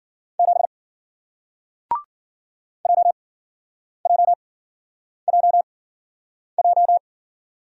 Morse code, heard as a keyed tone, about 53 wpm, spaced as for 9 wpm, 700 Hz. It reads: 7 JJJJ